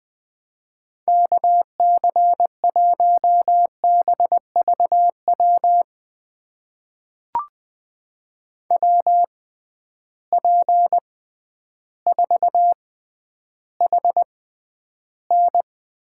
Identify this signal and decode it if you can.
Morse code, heard as a keyed tone, about 21 wpm, spaced as for 11 wpm, 700 Hz. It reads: KC1BVW WP4HN
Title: USA Call Signs - 6 Characters - 6x Character Spacing - 20wpm